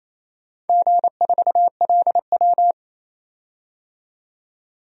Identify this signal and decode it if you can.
Morse code, heard as a keyed tone, about 28 wpm, 700 Hz. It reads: G4LW
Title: DXCC-100 Call Signs - 4 Characters - 28wpm